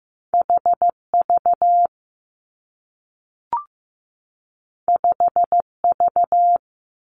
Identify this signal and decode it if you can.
Morse code, heard as a keyed tone, about 15 wpm, 700 Hz. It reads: HV 5V
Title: Two Challenging Letter-Number Combinations 15wpm